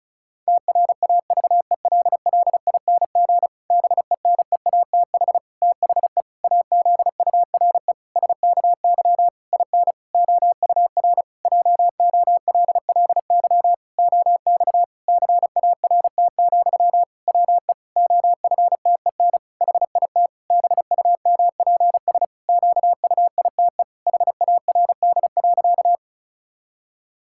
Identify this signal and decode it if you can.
Morse code, heard as a keyed tone, about 36 wpm, 700 Hz. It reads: TRAVELLING BENEATH THE AZURE SKY IN OUR JOLLY OX CART, WE OFTEN HIT BUMPS QUITE HARD.
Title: Pangram Challenge v2 - 35wpm